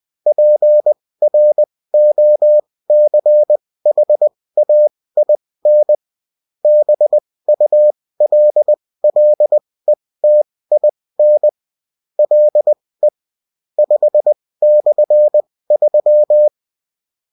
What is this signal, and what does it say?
Morse code, heard as a keyed tone, about 20 wpm, spaced as for 16 wpm, 600 Hz. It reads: PROCHAIN BULLETIN LE 5/3